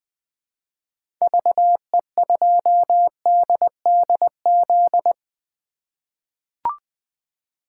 Morse code, heard as a keyed tone, about 20 wpm, 700 Hz. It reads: VE2DDZ
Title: 2023 SOTA Chaser Call Signs - 20wpm